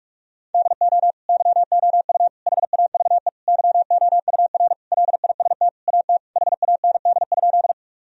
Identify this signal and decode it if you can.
Morse code, heard as a keyed tone, about 46 wpm, 700 Hz. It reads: DO YOU HAVE YOUR LIST AT HAND?
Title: Sentences from Top 1000 Words - Review Part 2 - 45wpm